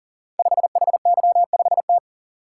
Morse code, heard as a keyed tone, about 42 wpm, 700 Hz. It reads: 5HY5T
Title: Code Groups - 5 Letter:Numbers - 40wpm